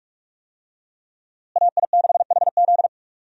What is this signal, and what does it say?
Morse code, heard as a keyed tone, about 43 wpm, 700 Hz. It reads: AI6HZ
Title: USA Call Signs - 5 Characters - Repeated 3x - 45wpm